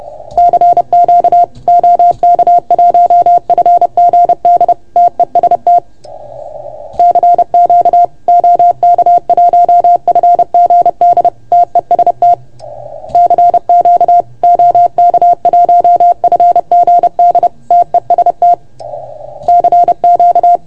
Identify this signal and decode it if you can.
Morse code, heard as a keyed tone, about 30 wpm, 690 Hz. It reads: CQ OK1FGD TEST CQ OK1FGD TEST CQ OK1FGD TEST CQ
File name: Archiv / CUC381 / OK1FGD CUC 381